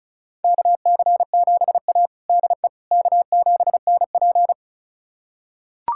Morse code, heard as a keyed tone, about 36 wpm, 700 Hz. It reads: KC7A DE K7NP
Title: USA Callsign DE Callsign - 4 Characters - 35wpm